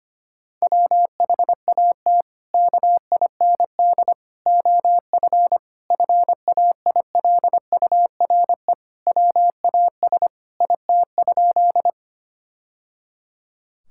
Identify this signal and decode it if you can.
Morse code, heard as a keyed tone, about 25 wpm, 700 Hz. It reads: WHAT KIND OF FAILURE WAS IT?
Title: Sentences from Top 1000 Words - Review Part 6 - 25wpm